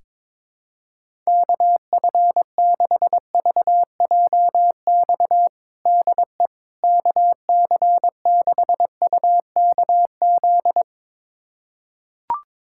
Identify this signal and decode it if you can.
Morse code, heard as a keyed tone, about 22 wpm, 700 Hz. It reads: KF6VJX DE KC6UKZ
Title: USA Callsign DE Callsign - 6 Characters - 22wpm